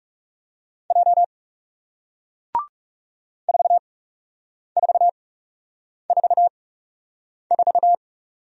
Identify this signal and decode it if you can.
Morse code, heard as a keyed tone, about 41 wpm, spaced as for 8 wpm, 700 Hz. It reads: J 4444